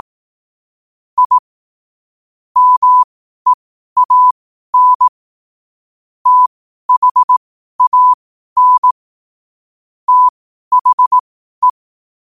Morse code, heard as a keyed tone, about 18 wpm, spaced as for 12 wpm, 1000 Hz. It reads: I MEAN THAN THE
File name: qotdfiles / 12wpm